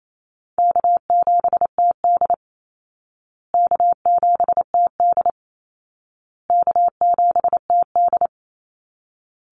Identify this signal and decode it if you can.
Morse code, heard as a keyed tone, about 28 wpm, 700 Hz. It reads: K7TD K7TD K7TD
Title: Easy Call Signs Repeated 3x - 28wpm